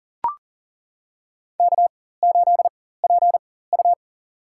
Morse code, heard as a keyed tone, about 40 wpm, spaced as for 19 wpm, 700 Hz. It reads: K8PU
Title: USA Call Signs - 4 Characters - 4x Character Spacing - 40wpm